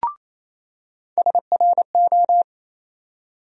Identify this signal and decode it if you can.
Morse code, heard as a keyed tone, about 27 wpm, 700 Hz. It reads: SRO
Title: USA Call Sign Postfixes - 3 Letters 28wpm